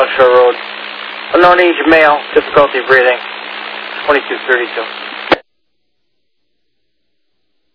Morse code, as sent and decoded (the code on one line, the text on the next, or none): none